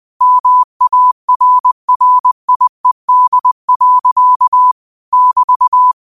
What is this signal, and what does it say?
Morse code, heard as a keyed tone, about 20 wpm, 1000 Hz. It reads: MARRIED. <BT>